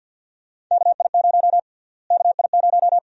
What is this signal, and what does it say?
Morse code, heard as a keyed tone, about 49 wpm, 700 Hz. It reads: KI0 KI0
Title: USA Call Sign Prefixes - Repeated 2x - 50wpm